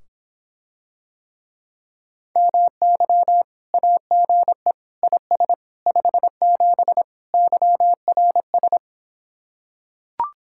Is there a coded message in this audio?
Yes: Morse code from 2.4 to 8.8 s, heard as a keyed tone, about 26 wpm, 700 Hz. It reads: MY AGE IS 57 YRS